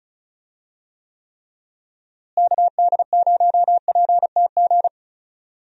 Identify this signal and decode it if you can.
Morse code, heard as a keyed tone, about 34 wpm, 700 Hz. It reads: KD0PTG
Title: USA Call Signs - 6 Characters - Repeated 3x - 35wpm